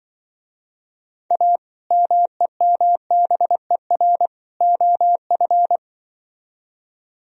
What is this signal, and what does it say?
Morse code, heard as a keyed tone, about 23 wpm, 700 Hz. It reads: A MEMBER OF
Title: Speed-Racing - Sets of 3 Words 20wpm